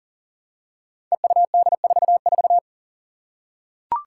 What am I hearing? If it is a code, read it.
Morse code, heard as a keyed tone, about 42 wpm, 700 Hz. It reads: EUD44
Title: Five Challenging Letter-Number Combinations 40wpm